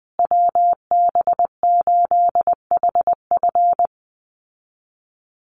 Morse code, heard as a keyed tone, about 20 wpm, 700 Hz. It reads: WB8HF
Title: Call Signs Repeated 3x - 20wpm